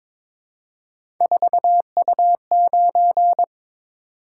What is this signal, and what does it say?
Morse code, heard as a keyed tone, about 22 wpm, 700 Hz. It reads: 4U9
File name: practice / 22wpm / Three Letter-Number Combinations - 22wpm